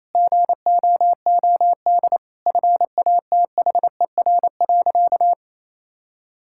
Morse code, heard as a keyed tone, about 28 wpm, 700 Hz. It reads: GOOD FATHER.
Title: Sentences from Top 1000 Words - Review Part 5 - 28wpm